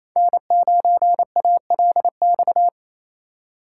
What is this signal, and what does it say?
Morse code, heard as a keyed tone, about 28 wpm, 700 Hz. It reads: N9ALX
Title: USA Call Signs - 5 Characters - 4x Character Spacing - 28wpm